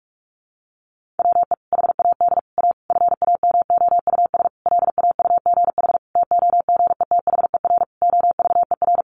Morse code, heard as a keyed tone, about 44 wpm, 700 Hz. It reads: WE HAD A FAMOUS LAUGH TOGETHER OVER
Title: Sentences from Top 1500 Words 45wpm